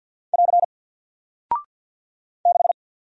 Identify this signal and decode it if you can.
Morse code, heard as a keyed tone, about 48 wpm, 700 Hz. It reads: <AR> 6
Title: Single Character (Letter:Number:Punctuation:Prosign) Alt 50wpm